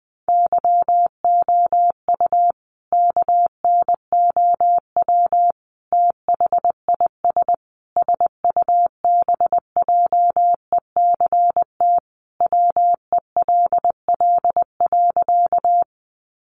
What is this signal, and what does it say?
Morse code, heard as a keyed tone, about 20 wpm, 700 Hz. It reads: YOU KNOW THIS SUBJECT WELL.
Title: Sentences from Top 800 Words 20wpm